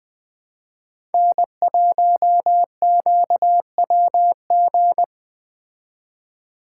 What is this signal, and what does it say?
Morse code, heard as a keyed tone, about 20 wpm, 700 Hz. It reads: N1QWG
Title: Speed Racing - USA 1x3 Call Signs 17wpm